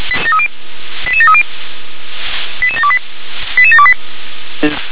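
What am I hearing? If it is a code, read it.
Five-tone selective call (ZVEI), heard as quick runs of five tones, tones of about 70 ms. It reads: 88730 88728 88728
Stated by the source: ZVEI five-tone selective call received by SDR.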